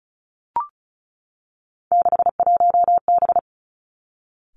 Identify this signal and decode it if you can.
Morse code, heard as a keyed tone, about 34 wpm, 700 Hz. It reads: 61B
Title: Three Challenging Letter-Number Combinations 35wpm